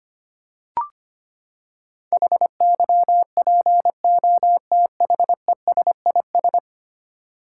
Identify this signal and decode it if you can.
Morse code, heard as a keyed tone, about 25 wpm, 700 Hz. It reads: HYPOTHESIS
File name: Ten Letter Words - 25wpm